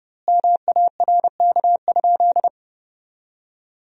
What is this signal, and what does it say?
Morse code, heard as a keyed tone, about 30 wpm, 700 Hz. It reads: MARK?
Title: 2x Word-Spacing - Sentences from Top 300 Words - 30wpm